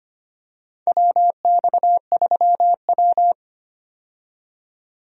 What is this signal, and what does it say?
Morse code, heard as a keyed tone, about 25 wpm, 700 Hz. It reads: WX3W
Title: USA Call Signs - 4 Characters - 25wpm